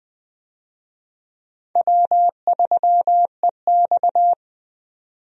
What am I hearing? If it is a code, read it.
Morse code, heard as a keyed tone, about 20 wpm, 700 Hz. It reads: W3EX
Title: USA Call Signs - 4 Characters - 20wpm